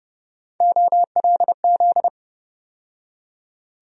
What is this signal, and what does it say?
Morse code, heard as a keyed tone, about 30 wpm, 700 Hz. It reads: OLZ
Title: USA Call Sign Postfixes - 3 Letters 30wpm